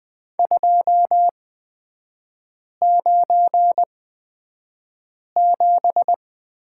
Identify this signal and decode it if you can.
Morse code, heard as a keyed tone, about 20 wpm, spaced as for 5 wpm, 700 Hz. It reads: 297
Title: 250 Lesson 40e - BK - Three Characters 20wpm